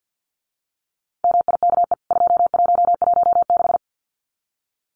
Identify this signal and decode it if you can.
Morse code, heard as a keyed tone, about 50 wpm, 700 Hz. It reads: MIKE 2116
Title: CWT Exchange 50wpm